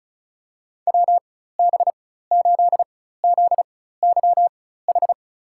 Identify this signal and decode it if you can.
Morse code, heard as a keyed tone, about 36 wpm, spaced as for 16 wpm, 700 Hz. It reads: WB8ZYH
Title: USA Call Signs - 6 Characters - 4x Character Spacing - 35wpm